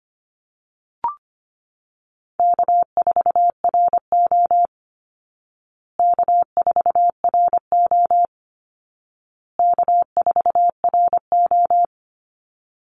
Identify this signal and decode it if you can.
Morse code, heard as a keyed tone, about 25 wpm, 700 Hz. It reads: K4RO K4RO K4RO
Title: Easy Call Signs Repeated 3x - 25wpm